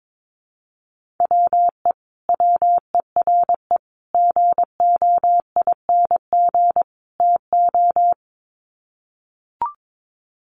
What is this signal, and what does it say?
Morse code, heard as a keyed tone, about 22 wpm, 700 Hz. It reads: WE WERE GOING TO
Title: Sets of 4 Words 22wpm